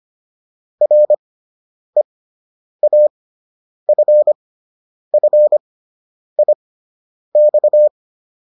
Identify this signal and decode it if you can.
Morse code, heard as a keyed tone, about 25 wpm, spaced as for 9 wpm, 600 Hz. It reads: REAFFIX